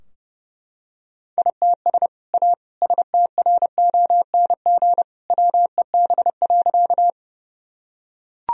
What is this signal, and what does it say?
Morse code, heard as a keyed tone, about 30 wpm, 700 Hz. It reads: ITS A STRONG WEB.
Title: Sentences from Top 1000 Words - Review Part 4 - 30wpm